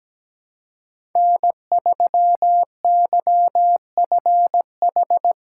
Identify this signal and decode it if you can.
Morse code, heard as a keyed tone, about 17 wpm, 700 Hz. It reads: N3YFH
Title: USA Call Signs - 5 Characters - 6x Character Spacing - 17wpm